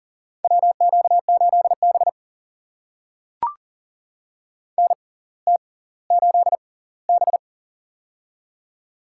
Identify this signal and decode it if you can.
Morse code, heard as a keyed tone, about 40 wpm, spaced as for 20 wpm, 700 Hz. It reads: WQ8B NT8B
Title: USA Call Signs - 4 Characters - 6x Character Spacing - 40wpm